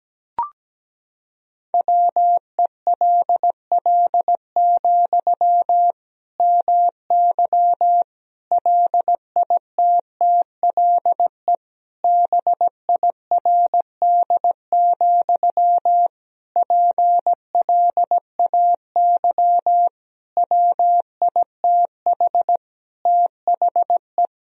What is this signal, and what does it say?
Morse code, heard as a keyed tone, about 17 wpm, 700 Hz. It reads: WELL, MY LITTLE BIRD, PLAY WITH THE
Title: Sentences from Top 1000 Words - Review Part 2 - 17wpm